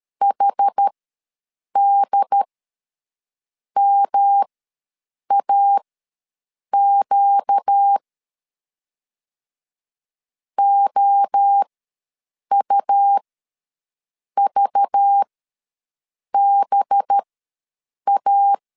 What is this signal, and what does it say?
Morse code, heard as a keyed tone, about 13 wpm, spaced as for 6 wpm, 800 Hz. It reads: HDMAQ OUVBA
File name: LEZIONI / 5 LBVC / 5 LBV ws 12wpm 3015